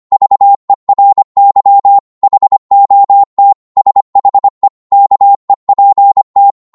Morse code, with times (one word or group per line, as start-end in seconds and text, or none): 0.1-2.0 VERY
2.2-3.5 HOT
3.8-4.7 SHE
4.9-6.5 KEPT